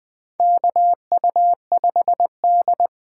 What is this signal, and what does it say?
Morse code, heard as a keyed tone, about 20 wpm, 700 Hz. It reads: KU5D
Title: USA Callsign DE Callsign - 4 Characters - 20wpm